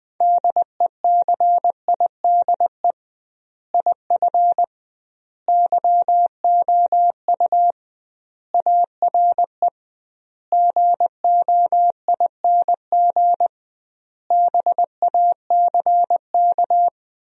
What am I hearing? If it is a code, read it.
Morse code, heard as a keyed tone, about 20 wpm, 700 Hz. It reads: DECIDE IF YOU ARE GOING BACK
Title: 2x Word-Spacing - Sentences from Top 500 Words - 20wpm